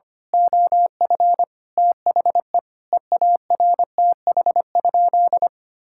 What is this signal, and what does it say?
Morse code, heard as a keyed tone, about 25 wpm, 700 Hz. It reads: OF THE EARTH?
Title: Sentences from Top 1600 Words 25wpm